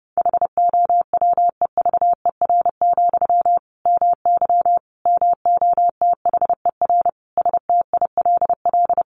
Morse code, heard as a keyed tone, about 28 wpm, 700 Hz. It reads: HOWEVER, MY MOTHER STILL
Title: Sentences from Top 1000 Words 30wpm